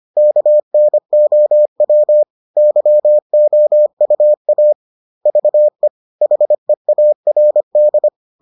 Morse code, heard as a keyed tone, about 25 wpm, 600 Hz. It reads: KNOW YOUA VE HEARD